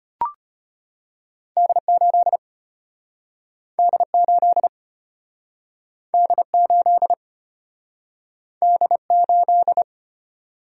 Morse code, heard as a keyed tone, about 34 wpm, 700 Hz. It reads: D8 D8 D8 D8